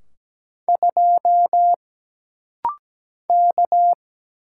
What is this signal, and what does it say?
Morse code, heard as a keyed tone, about 17 wpm, 700 Hz. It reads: 2 K
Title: Single Letter-Number - Rapid-Fire 17wpm